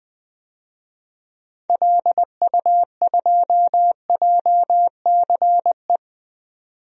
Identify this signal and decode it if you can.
Morse code, heard as a keyed tone, about 20 wpm, 700 Hz. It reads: LU2JCE
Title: DXCC-100 Call Signs - 6 Characters - 20wpm